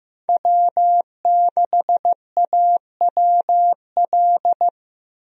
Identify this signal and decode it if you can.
Morse code, heard as a keyed tone, about 15 wpm, 700 Hz. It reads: W6AWL
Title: USA Call Signs - 5 Characters - Repeated 2x - 15wpm